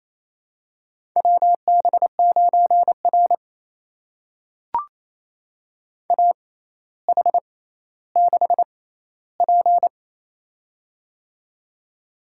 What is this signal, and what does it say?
Morse code, heard as a keyed tone, about 27 wpm, spaced as for 14 wpm, 700 Hz. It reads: WB9R AH6P
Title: USA Call Signs - 4 Characters - 6x Character Spacing - 28wpm